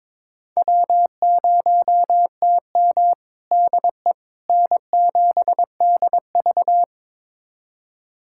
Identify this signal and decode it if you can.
Morse code, heard as a keyed tone, about 21 wpm, 700 Hz. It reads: W0TM DE N7DV
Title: USA Callsign DE Callsign - 4 Characters - 22wpm